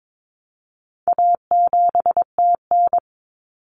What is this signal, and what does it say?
Morse code, heard as a keyed tone, about 22 wpm, 700 Hz. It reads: A7TN
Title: Common Cut Numbers - 4-Digit 22wpm